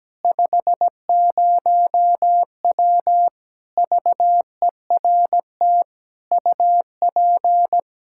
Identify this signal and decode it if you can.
Morse code, heard as a keyed tone, about 17 wpm, 700 Hz. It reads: 50W VERT UP
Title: Practice QSO - Exchange 2 - Info - 17wpm